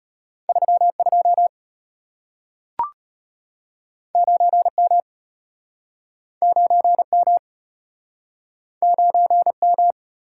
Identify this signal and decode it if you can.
Morse code, heard as a keyed tone, about 34 wpm, 700 Hz. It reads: 32 9M 9M 9M